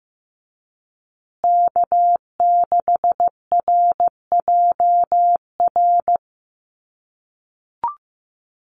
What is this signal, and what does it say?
Morse code, heard as a keyed tone, about 15 wpm, 700 Hz. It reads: K6RJR
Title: USA Call Signs Repeated 2x 15wpm